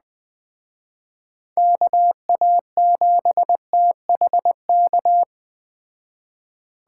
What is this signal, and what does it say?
Morse code, heard as a keyed tone, about 21 wpm, 700 Hz. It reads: KA7THK